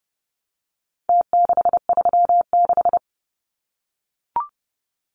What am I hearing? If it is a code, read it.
Morse code, heard as a keyed tone, about 30 wpm, 700 Hz. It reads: T636
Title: Numbers - Contest Style - 4-Digit 30wpm